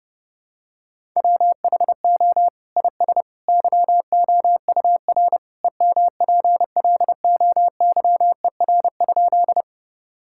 Morse code, heard as a keyed tone, about 30 wpm, 700 Hz. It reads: WHO IS YOUR EMPLOYER?
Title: Sentences from Top 1600 Words 30wpm